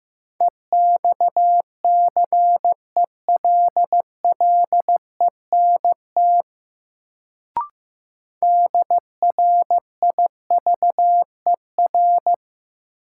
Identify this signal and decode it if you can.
Morse code, heard as a keyed tone, about 15 wpm, 700 Hz. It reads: EXCELLENT DRIVER
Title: Sentences from Top 1000 Words - New Word Drill - 15wpm